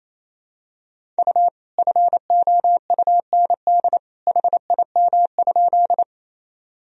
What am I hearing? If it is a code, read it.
Morse code, heard as a keyed tone, about 28 wpm, 700 Hz. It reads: U FOUND HIM?
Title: Sentences from Top 300 Words and Abbreviations 28wpm